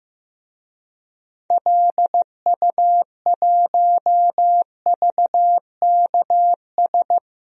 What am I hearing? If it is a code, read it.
Morse code, heard as a keyed tone, about 15 wpm, 700 Hz. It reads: LU1VKS
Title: DXCC-100 Call Signs - 6 Characters - 15wpm